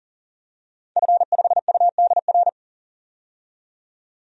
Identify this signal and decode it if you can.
Morse code, heard as a keyed tone, about 38 wpm, 700 Hz. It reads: F5UDR
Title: DXCC-100 Call Signs - 5 Characters - 40wpm